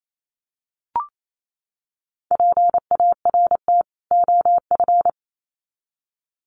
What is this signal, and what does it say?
Morse code, heard as a keyed tone, about 28 wpm, 700 Hz. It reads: PART OF
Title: Sets of 2 Words 28wpm